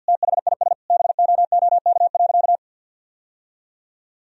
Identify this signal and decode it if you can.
Morse code, heard as a keyed tone, about 50 wpm, 700 Hz. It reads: THIS BOOK.
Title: Sentences from Top 1000 Words - Review Part 4 - 50wpm